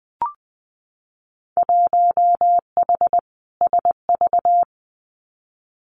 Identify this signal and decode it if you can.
Morse code, heard as a keyed tone, about 20 wpm, 700 Hz. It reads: 1H SV
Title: Winter Field Day Exchanges 20wpm